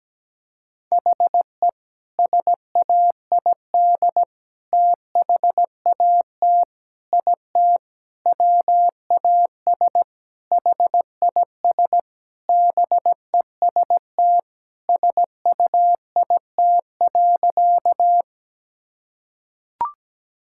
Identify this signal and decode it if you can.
Morse code, heard as a keyed tone, about 17 wpm, 700 Hz. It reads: HE SAID THAT IT WAS HIS BEST SUIT.
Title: Sentences from Top 1600 Words 17wpm